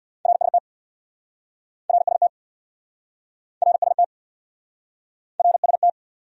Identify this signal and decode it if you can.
Morse code, heard as a keyed tone, about 69 wpm, 700 Hz. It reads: RST RST RST RST